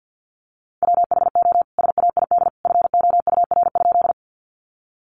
Morse code, heard as a keyed tone, about 50 wpm, 700 Hz. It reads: WHO SAID FOUR?
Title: Sentences from Top 300 Words 50wpm